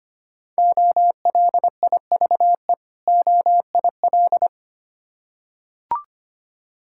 Morse code, heard as a keyed tone, about 25 wpm, 700 Hz. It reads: OLIVE OIL
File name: practice / 25wpm / Difficult Words with Hints - 25wpm